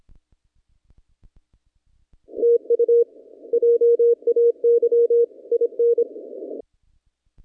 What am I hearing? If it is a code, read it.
Morse code, heard as a keyed tone, about 23 wpm, 470 Hz. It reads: TU JAY IN
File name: contests / naqp-cw-k5tr-op-k5pi- -0300